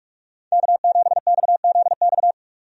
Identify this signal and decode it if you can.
Morse code, heard as a keyed tone, about 46 wpm, 700 Hz. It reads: K7XZX
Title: USA Call Signs - 5 Characters - 45wpm